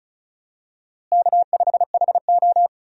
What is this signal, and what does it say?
Morse code, heard as a keyed tone, about 36 wpm, 700 Hz. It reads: K5HO
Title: USA Call Signs - 4 Characters - Repeated 3x - 35wpm